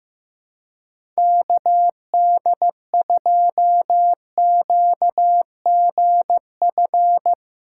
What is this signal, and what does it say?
Morse code, heard as a keyed tone, about 15 wpm, 700 Hz. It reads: KD2QGF
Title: USA Callsign DE Callsign - 6 Characters - 15wpm